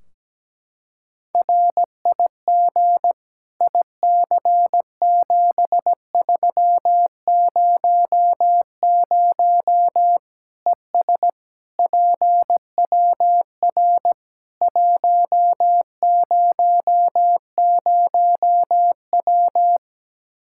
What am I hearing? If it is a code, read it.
Morse code, heard as a keyed tone, about 17 wpm, 700 Hz. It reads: RIG IC7300 ES PWR 100W
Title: Practice QSO - Exchange 2 - Info - 17wpm